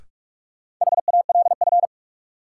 Morse code, heard as a keyed tone, about 46 wpm, 700 Hz. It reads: HALF
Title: Sets of 3 Words - New Word Drill - 45wpm